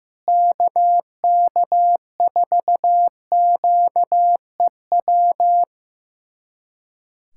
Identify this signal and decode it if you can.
Morse code, heard as a keyed tone, about 16 wpm, 700 Hz. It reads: KK4QEW
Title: USA Callsign DE Callsign 15wpm